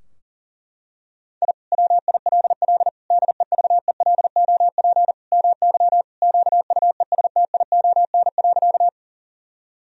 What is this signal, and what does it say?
Morse code, heard as a keyed tone, about 37 wpm, 700 Hz. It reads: I WILL DEVELOP MY QUESTION.
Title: Sentences from Top 1000 Words - Review Part 2 - 40wpm